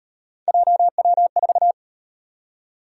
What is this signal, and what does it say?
Morse code, heard as a keyed tone, about 35 wpm, 700 Hz. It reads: JW4